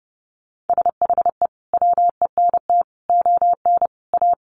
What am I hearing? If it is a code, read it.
Morse code, heard as a keyed tone, about 30 wpm, 700 Hz. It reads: SHE WENT ON A
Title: Sentences from Top 1400 Words 30wpm